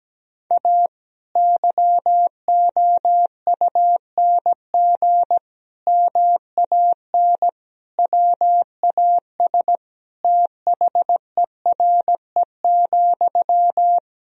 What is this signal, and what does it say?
Morse code, heard as a keyed tone, about 17 wpm, 700 Hz. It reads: A YOUNG MAN WAS THERE,
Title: Sentences from Top 1000 Words - Review Part 3 - 17wpm